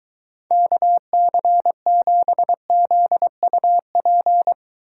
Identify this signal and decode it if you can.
Morse code, heard as a keyed tone, about 23 wpm, 700 Hz. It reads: KC7ZUP